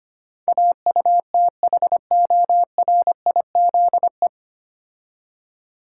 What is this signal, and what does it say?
Morse code, heard as a keyed tone, about 25 wpm, 690 Hz. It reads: AUTHORIZE
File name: practice / 25wpm / 9 Letter Words - First 3 Letters Given - 25wpm